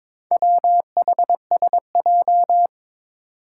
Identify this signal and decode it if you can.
Morse code, heard as a keyed tone, about 21 wpm, 700 Hz. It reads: WHSJ
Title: Four Challenging Letter-Number Combinations 22wpm